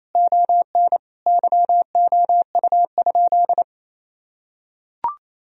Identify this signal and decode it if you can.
Morse code, heard as a keyed tone, about 28 wpm, 700 Hz. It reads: ON YOU?
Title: Sentences from Top 1000 Words - Review Part 3 - 28wpm